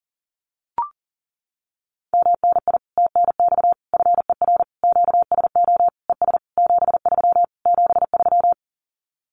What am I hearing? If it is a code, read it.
Morse code, heard as a keyed tone, about 40 wpm, 700 Hz. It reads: MNI TNX FER QSO ES 73 73
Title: Common QSO Phrases Encore 40wpm